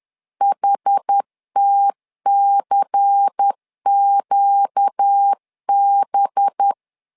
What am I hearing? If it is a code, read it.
Morse code, heard as a keyed tone, about 10 wpm, 800 Hz. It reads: HTCQB
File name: LEZIONI / 5 LBVC / 5 LBVC 10wpm 2349